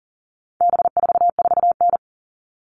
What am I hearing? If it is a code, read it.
Morse code, heard as a keyed tone, about 40 wpm, 700 Hz. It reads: B44N
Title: All Cut Numbers - 4-Digit 40wpm